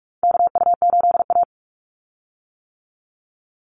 Morse code, heard as a keyed tone, about 44 wpm, 700 Hz. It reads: KU8A
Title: USA Call Signs 45wpm